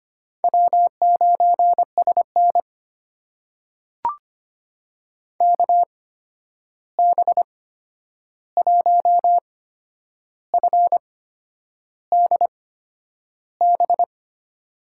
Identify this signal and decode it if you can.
Morse code, heard as a keyed tone, about 25 wpm, spaced as for 9 wpm, 700 Hz. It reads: W9SN KB1FDB
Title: USA Call Signs - 8x Character Spacing - 25wpm